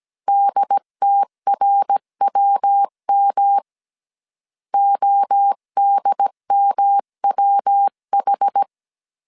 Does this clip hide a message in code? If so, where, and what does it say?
Yes: Morse code from 0.3 to 8.6 s, heard as a keyed tone, about 17 wpm, 800 Hz. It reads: DTRWM ODMWH